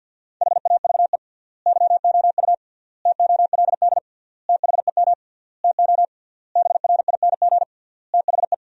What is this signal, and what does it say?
Morse code, heard as a keyed tone, about 53 wpm, 700 Hz. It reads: HAVE YOU TOLD THEM TO BRING THE
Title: 3x Word-Spacing - Sentences from Top 500 Words - 50wpm